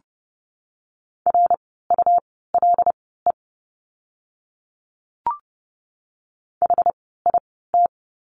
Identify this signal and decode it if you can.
Morse code, heard as a keyed tone, about 30 wpm, spaced as for 17 wpm, 700 Hz. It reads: RULE HIT
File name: Lesson 17f - Letter u - Words 30wpm